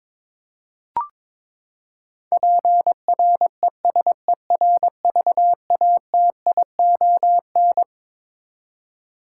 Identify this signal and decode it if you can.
Morse code, heard as a keyed tone, about 22 wpm, 700 Hz. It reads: PRESERVATION
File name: Twelve Letter Words - 22wpm